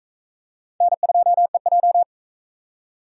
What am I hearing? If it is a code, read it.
Morse code, heard as a keyed tone, about 41 wpm, 700 Hz. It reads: N2EJ